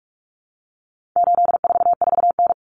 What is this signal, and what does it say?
Morse code, heard as a keyed tone, about 44 wpm, 700 Hz. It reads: 844N
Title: Common Cut Numbers - 4-Digit 45wpm